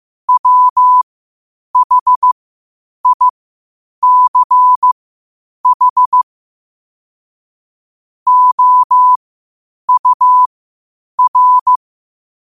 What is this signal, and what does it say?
Morse code, heard as a keyed tone, about 15 wpm, spaced as for 8 wpm, 1000 Hz. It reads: WHICH OUR
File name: qotdfiles / 8wpm